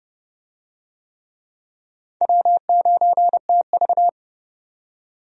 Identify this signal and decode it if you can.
Morse code, heard as a keyed tone, about 30 wpm, 700 Hz. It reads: W9TV
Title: USA Call Signs - 4 Characters - 30wpm